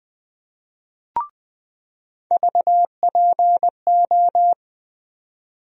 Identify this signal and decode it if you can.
Morse code, heard as a keyed tone, about 20 wpm, 700 Hz. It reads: VPO
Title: USA Call Sign Postfixes - 3 Letters 20wpm